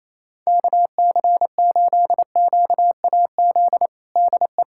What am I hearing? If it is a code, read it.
Morse code, heard as a keyed tone, about 28 wpm, 700 Hz. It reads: KC8QAZ DE
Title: USA Callsign DE Callsign - 6 Characters - 28wpm